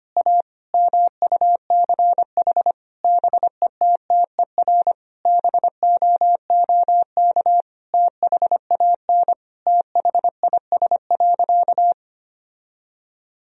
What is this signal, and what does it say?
Morse code, heard as a keyed tone, about 25 wpm, 700 Hz. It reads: A MUCH BETTER BOOK THAN THIS.
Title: Sentences from Top 1000 Words - Review Part 3 - 25wpm